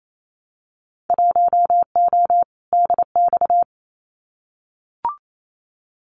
Morse code, heard as a keyed tone, about 28 wpm, 700 Hz. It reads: 1O DX
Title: Winter Field Day Exchanges 28wpm